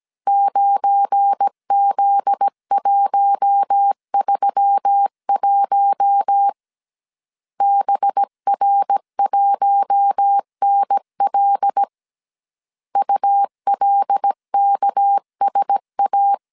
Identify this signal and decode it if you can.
Morse code, heard as a keyed tone, about 17 wpm, 800 Hz. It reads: 9Z131 BR1NL ULKSA
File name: CW / DET / 1 9 1500C S A 16wpm 2828